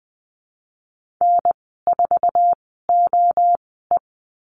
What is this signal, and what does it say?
Morse code, heard as a keyed tone, about 20 wpm, spaced as for 15 wpm, 700 Hz. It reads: N4OE
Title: 048 Lesson 9g - Number 4 - Call Signs 20wpm